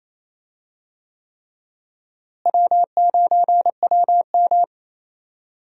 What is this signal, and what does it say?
Morse code, heard as a keyed tone, about 28 wpm, 700 Hz. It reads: W9WM
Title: USA Call Signs - 4 Characters - Repeated 3x - 28wpm